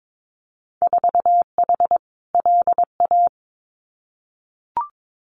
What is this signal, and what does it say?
Morse code, heard as a keyed tone, about 22 wpm, 700 Hz. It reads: 4H LA
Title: Winter Field Day Exchanges Encore 22wpm